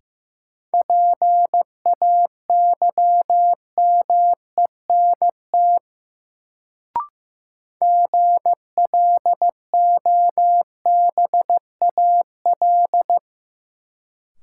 Morse code, heard as a keyed tone, about 15 wpm, 700 Hz. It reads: PAYMENT GLOBAL
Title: Sentences from Top 1200 Words - New Word Drill Alt - 15wpm